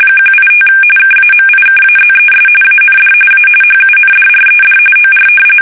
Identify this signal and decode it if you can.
RTTY, about 45.45 baud. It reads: 185 KAMIKAZES AND A HUNDRED OR
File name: fsk850 / battleshipnj